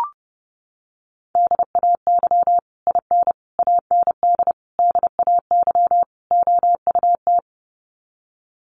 Morse code, heard as a keyed tone, about 30 wpm, 700 Hz. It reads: DAY IN AND DAY OUT
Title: Sets of 5 Words Encore2 30wpm